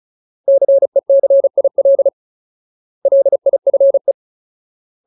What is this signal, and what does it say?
Morse code, heard as a keyed tone, about 34 wpm, 550 Hz. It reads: CECIL LIFE
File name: Ultimate Word List / 35WPM Ultimate